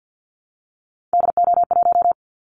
Morse code, heard as a keyed tone, about 48 wpm, 700 Hz. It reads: DO1
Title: Callsign Prefixes for Top 25 DXCC - Repeated 3x - 50wpm